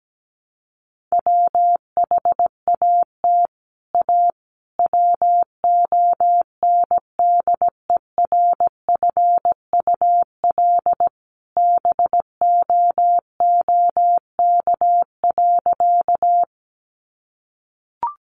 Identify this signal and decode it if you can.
Morse code, heard as a keyed tone, about 17 wpm, 700 Hz. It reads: WHAT A WONDERFUL BOOK.
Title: Sentences from Top 1100 Words 17wpm